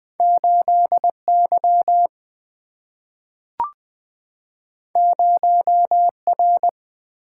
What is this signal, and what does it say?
Morse code, heard as a keyed tone, about 20 wpm, 700 Hz. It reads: 8 Y 0 R
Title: Single-Letter Single-Number 20wpm